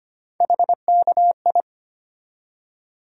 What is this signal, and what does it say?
Morse code, heard as a keyed tone, about 25 wpm, 700 Hz. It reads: HKI
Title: USA Call Sign Postfixes - 3 Letters 25wpm